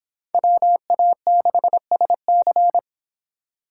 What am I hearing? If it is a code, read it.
Morse code, heard as a keyed tone, about 25 wpm, 700 Hz. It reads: WA6SC